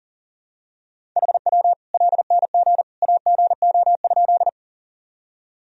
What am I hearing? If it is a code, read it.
Morse code, heard as a keyed tone, about 40 wpm, 700 Hz. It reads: HW LNG AGO?
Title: Sentences from Top 500 Words and Abbreviations 40wpm